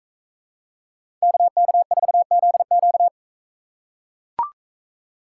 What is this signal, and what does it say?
Morse code, heard as a keyed tone, about 44 wpm, 700 Hz. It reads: KK4ZQ